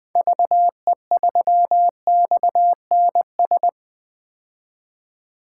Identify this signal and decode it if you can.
Morse code, heard as a keyed tone, about 20 wpm, 700 Hz. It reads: VE3XNS
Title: 2023 POTA Activator Call Signs - 20wpm